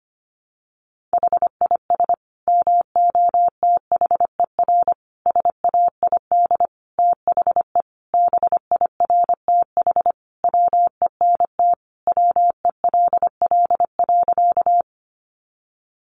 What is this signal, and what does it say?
Morse code, heard as a keyed tone, about 25 wpm, 700 Hz. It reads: HIS MOTHER SAID THE BIRTH WENT WELL.
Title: Sentences from Top 800 Words 25wpm